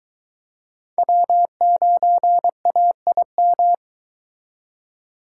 Morse code, heard as a keyed tone, about 22 wpm, 700 Hz. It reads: W9AIM